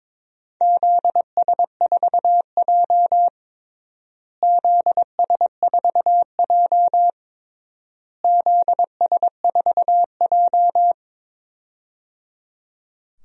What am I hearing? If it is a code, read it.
Morse code, heard as a keyed tone, about 22 wpm, 700 Hz. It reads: ZS4J ZS4J ZS4J
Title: DXCC-100 Call Signs - 4 Characters - Repeated 3x - 22wpm